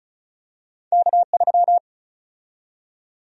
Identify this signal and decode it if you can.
Morse code, heard as a keyed tone, about 34 wpm, 700 Hz. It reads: K3
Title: USA Call Sign Prefixes - Repeated 2x - 35wpm